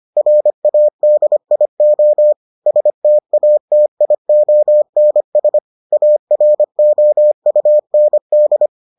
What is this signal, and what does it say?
Morse code, heard as a keyed tone, about 25 wpm, 600 Hz. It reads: RADIO STATIONS AROUND